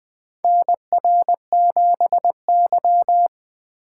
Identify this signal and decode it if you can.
Morse code, heard as a keyed tone, about 20 wpm, 700 Hz. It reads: NR7Y
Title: 2023 SOTA Chaser Call Signs - 20wpm